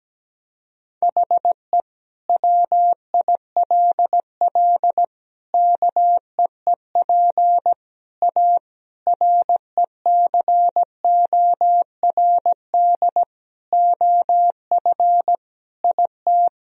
Morse code, heard as a keyed tone, about 17 wpm, 700 Hz. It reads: HE WILL KEEP A RECORD OF IT
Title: Sentences from Top 1000 Words - Review Part 3 - 17wpm